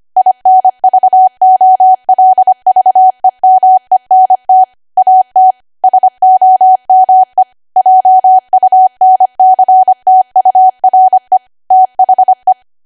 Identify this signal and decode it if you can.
Morse code, heard as a keyed tone, about 25 wpm, 750 Hz. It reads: INVOLVEMENT AT SOME JUNCTURE THE
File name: cwa / W1AW-Level3 / 25WPM